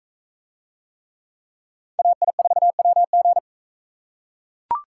Morse code, heard as a keyed tone, about 42 wpm, 700 Hz. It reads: AI4WG